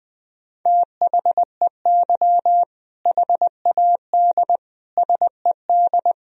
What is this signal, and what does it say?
Morse code, heard as a keyed tone, about 20 wpm, 700 Hz. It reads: THEY HAD SED
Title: Sentences from Top 400 Words and Abbreviations 20wpm